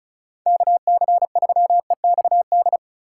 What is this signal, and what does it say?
Morse code, heard as a keyed tone, about 36 wpm, 700 Hz. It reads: KC3EXD
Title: USA Call Signs - 4x Character Spacing - 35wpm